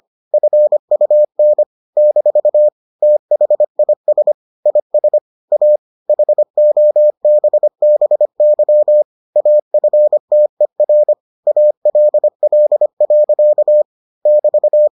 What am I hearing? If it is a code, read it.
Morse code, heard as a keyed tone, about 25 wpm, 600 Hz. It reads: FUN - THIS IS A HOBBY AFTER ALL. <BT>